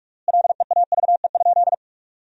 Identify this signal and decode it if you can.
Morse code, heard as a keyed tone, about 46 wpm, 700 Hz. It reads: LEAVE?
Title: 3x Word-Spacing - Sentences from Top 400 Words - 45wpm